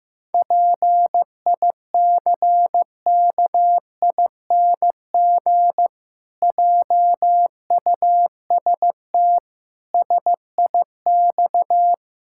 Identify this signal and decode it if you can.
Morse code, heard as a keyed tone, about 15 wpm, 700 Hz. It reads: PICKING JUST SIX
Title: Pangram Challenge v2 - 15wpm